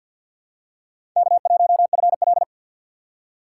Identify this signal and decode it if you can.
Morse code, heard as a keyed tone, about 46 wpm, 700 Hz. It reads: K1FL
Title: USA Call Signs - 2x Character Spacing - 50wpm